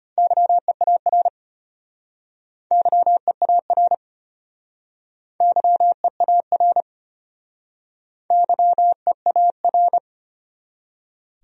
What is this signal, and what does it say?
Morse code, heard as a keyed tone, about 32 wpm, 700 Hz. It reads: YEAR YEAR YEAR YEAR